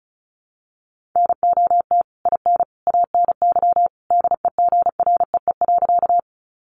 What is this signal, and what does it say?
Morse code, heard as a keyed tone, about 34 wpm, 700 Hz. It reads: NOT IN ANY DEGREE.
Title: Sentences from Top 1500 Words 35wpm